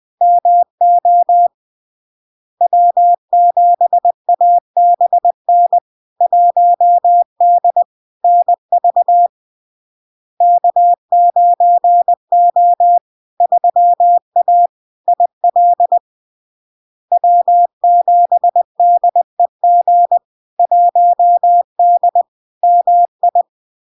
Morse code, heard as a keyed tone, about 20 wpm, 700 Hz. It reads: MO W7ABN 1D NV K9O 3A IL W7DEG 1D MI
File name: LARK-FD-2022-Log-in-CW-20wpm-700-Hz